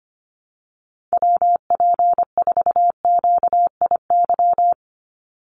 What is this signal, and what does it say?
Morse code, heard as a keyed tone, about 25 wpm, 700 Hz. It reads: WP4QIY
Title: USA Call Signs 25wpm